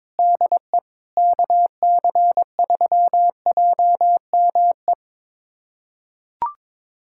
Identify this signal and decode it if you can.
Morse code, heard as a keyed tone, about 22 wpm, 700 Hz. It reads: DE KC3JME
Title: USA Callsign DE Callsign 22wpm